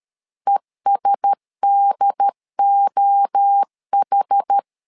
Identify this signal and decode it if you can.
Morse code, heard as a keyed tone, about 13 wpm, 800 Hz. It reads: ESDOH
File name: LEZIONI / 4 Q1S 12wpm 1802